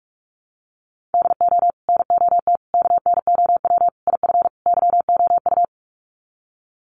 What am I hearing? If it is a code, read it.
Morse code, heard as a keyed tone, about 44 wpm, 700 Hz. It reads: DO NOT KNOW IF YOU
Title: Sets of 5 Words 45wpm